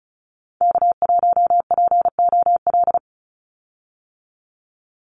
Morse code, heard as a keyed tone, about 34 wpm, 700 Hz. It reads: K1POL
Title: USA Call Signs 35wpm